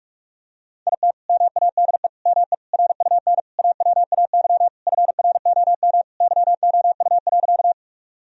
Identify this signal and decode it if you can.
Morse code, heard as a keyed tone, about 42 wpm, 700 Hz. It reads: IT MADE ME RUN AWAY FROM YOU.
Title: Sentences from Top 1000 Words - Review Part 1 - 45wpm